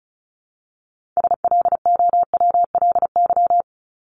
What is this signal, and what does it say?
Morse code, heard as a keyed tone, about 34 wpm, 700 Hz. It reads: SLOWLY
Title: Top 1001-1250 Words 35wpm